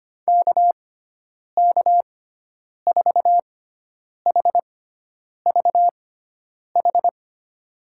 Morse code, heard as a keyed tone, about 25 wpm, spaced as for 9 wpm, 700 Hz. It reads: KK4HVH
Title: USA Call Signs - 6 Characters - 6x Character Spacing - 25wpm